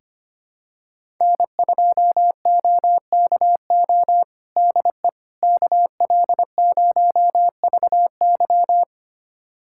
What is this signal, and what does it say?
Morse code, heard as a keyed tone, about 25 wpm, 700 Hz. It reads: N2OKO DE KL0VY
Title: USA Callsign DE Callsign 25wpm